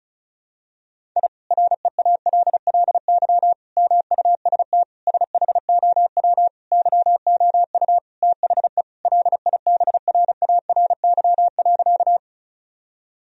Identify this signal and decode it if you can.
Morse code, heard as a keyed tone, about 36 wpm, 700 Hz. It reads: I REALLY MUST SHOW YOU THE LIBRARY.
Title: Sentences from Top 1000 Words - Review Part 4 - 35wpm